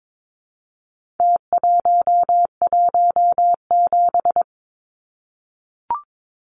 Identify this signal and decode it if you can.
Morse code, heard as a keyed tone, about 22 wpm, 700 Hz. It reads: T117
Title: Numbers - Contest Style - 4-Digit 22wpm